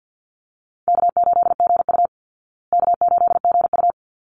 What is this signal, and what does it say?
Morse code, heard as a keyed tone, about 50 wpm, 700 Hz. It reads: K8GU K8GU
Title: Easy Call Signs Repeated 2x - 50wpm